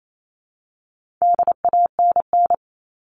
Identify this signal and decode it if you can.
Morse code, heard as a keyed tone, about 28 wpm, 700 Hz. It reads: DANN
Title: All Cut Numbers - 4-Digit 28wpm